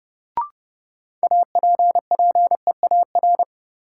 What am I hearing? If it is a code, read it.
Morse code, heard as a keyed tone, about 30 wpm, 700 Hz. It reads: APPEAR
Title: Sentences from Top 1000 Words - Review Part 2 - New Word Drill - 30wpm